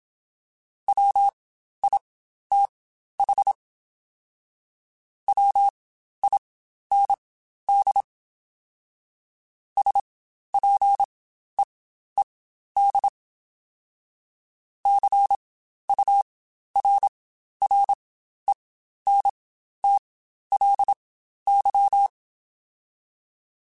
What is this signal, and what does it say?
Morse code, heard as a keyed tone, about 27 wpm, spaced as for 12 wpm, 800 Hz. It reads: WITH WIND SPEED CURRENTLY
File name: Spaced Characters / Spaced Characters-mp3s / Weather 5 12 sp @25